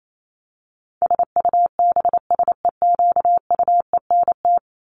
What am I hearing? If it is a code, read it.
Morse code, heard as a keyed tone, about 28 wpm, 700 Hz. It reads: SUBSEQUENT
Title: LY Suffix - Top 100 Words with Base Words 28wpm